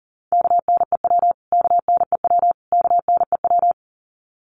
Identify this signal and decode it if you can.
Morse code, heard as a keyed tone, about 38 wpm, 700 Hz. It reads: KNEW KNEW KNEW
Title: Top 500 Words Repeated 40wpm